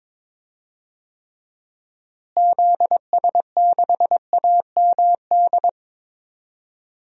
Morse code, heard as a keyed tone, about 22 wpm, 700 Hz. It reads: ZS6AMD
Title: DXCC-100 Call Signs - 6 Characters - Repeated 2x - 22wpm